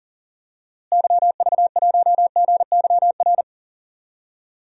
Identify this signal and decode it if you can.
Morse code, heard as a keyed tone, about 40 wpm, 700 Hz. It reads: YV1GYR
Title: DXCC-100 Call Signs - 6 Characters - Repeated 2x - 40wpm